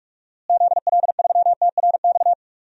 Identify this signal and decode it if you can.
Morse code, heard as a keyed tone, about 41 wpm, 700 Hz. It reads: ZL3TRX
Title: DXCC-100 Call Signs - 6 Characters - Repeated 3x - 45wpm